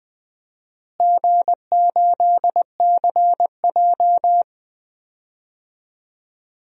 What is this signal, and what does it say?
Morse code, heard as a keyed tone, about 20 wpm, 700 Hz. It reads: G8CJ
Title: DXCC-100 Call Signs - 4 Characters - Repeated 3x - 20wpm